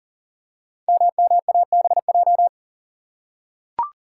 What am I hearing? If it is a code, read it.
Morse code, heard as a keyed tone, about 42 wpm, 700 Hz. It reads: MMABJ
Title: Code Groups - 5 Letter:Numbers - 40wpm